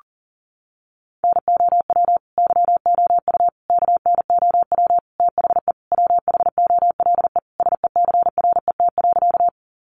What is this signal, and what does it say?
Morse code, heard as a keyed tone, about 40 wpm, 700 Hz. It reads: NOW YOU KNOW THE WHOLE SECRET.
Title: Sentences from Top 1200 Words 40wpm